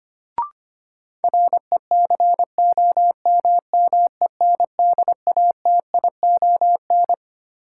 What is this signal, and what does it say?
Morse code, heard as a keyed tone, about 25 wpm, 700 Hz. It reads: RECOMMENDATION
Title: Sentences from Top 1400 Words - New Word Drill Alt - 25wpm